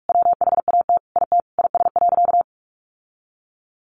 Morse code, heard as a keyed tone, about 44 wpm, 700 Hz. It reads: WHAT IT IS.
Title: Sentences from Top 1500 Words 45wpm